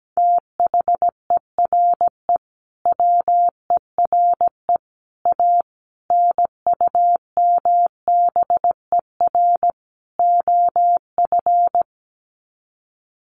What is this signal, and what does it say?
Morse code, heard as a keyed tone, about 17 wpm, 700 Hz. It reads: THERE WERE A NUMBER OF
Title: Sets of 5 Words Encore2 17wpm